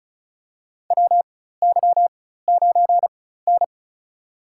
Morse code, heard as a keyed tone, about 36 wpm, spaced as for 16 wpm, 700 Hz. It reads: WY9N
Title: USA Call Signs - 4 Characters - 4x Character Spacing - 35wpm